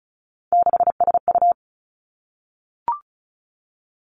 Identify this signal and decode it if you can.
Morse code, heard as a keyed tone, about 36 wpm, 700 Hz. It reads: 6SU
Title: Three Challenging Letter-Number Combinations 35wpm